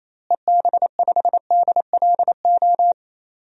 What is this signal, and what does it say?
Morse code, heard as a keyed tone, about 28 wpm, 700 Hz. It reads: EB5DLO
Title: DXCC-100 Call Signs - 6 Characters - Repeated 2x - 28wpm